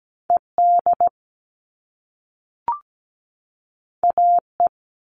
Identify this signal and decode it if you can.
Morse code, heard as a keyed tone, about 17 wpm, 700 Hz. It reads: ED AE
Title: All Cut Numbers - 2-Digit 17wpm